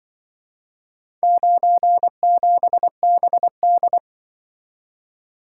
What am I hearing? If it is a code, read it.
Morse code, heard as a keyed tone, about 24 wpm, 700 Hz. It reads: 97BD
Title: Speed-Racing - Four Letter-Number Combinations - 20wpm